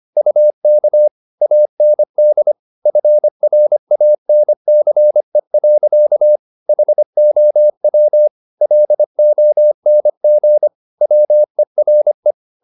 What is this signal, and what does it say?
Morse code, heard as a keyed tone, about 25 wpm, 600 Hz. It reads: UK AND FRANCE. HOW LONG WERE